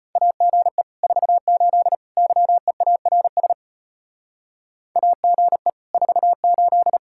Morse code, heard as a keyed tone, about 33 wpm, 700 Hz. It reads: AGE 48 YEARS AGE 48